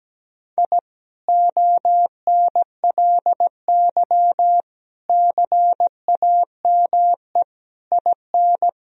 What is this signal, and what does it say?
Morse code, heard as a keyed tone, about 17 wpm, 700 Hz. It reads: I ONLY CAME IN
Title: Sentences from Top 1000 Words - Review Part 2 - 17wpm